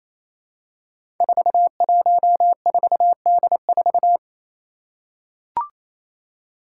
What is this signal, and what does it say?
Morse code, heard as a keyed tone, about 28 wpm, 700 Hz. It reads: 414D4
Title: Five Challenging Letter-Number Combinations 28wpm